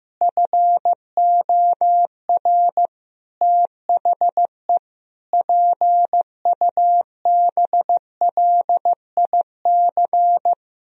Morse code, heard as a keyed tone, about 15 wpm, 700 Hz. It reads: FOR THE PUBLIC
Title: Sentences from Top 1000 Words - Review Part 3 - 15wpm